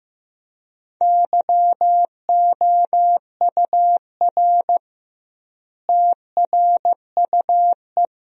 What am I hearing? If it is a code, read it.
Morse code, heard as a keyed tone, about 16 wpm, 700 Hz. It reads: YOUR TRUE
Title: 2x Word-Spacing - Sentences from Top 400 Words - 15wpm